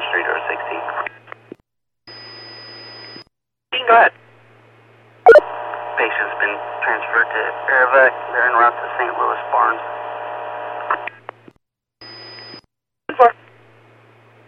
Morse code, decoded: none

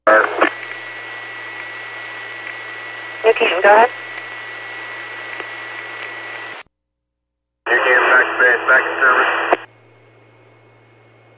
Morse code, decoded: none

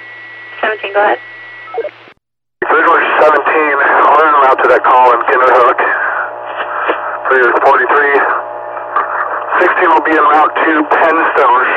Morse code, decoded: none

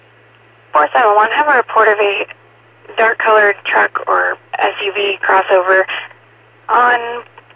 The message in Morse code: none

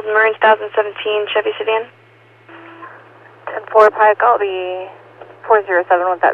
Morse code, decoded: none

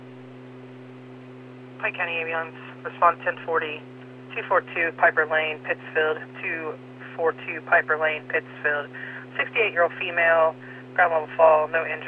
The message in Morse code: none

